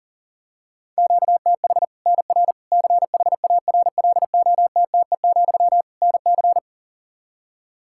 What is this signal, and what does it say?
Morse code, heard as a keyed tone, about 40 wpm, 700 Hz. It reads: QTH NR CHARLOTTE, NC
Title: Practice QSO - Exchange 1 - Beginning - 40wpm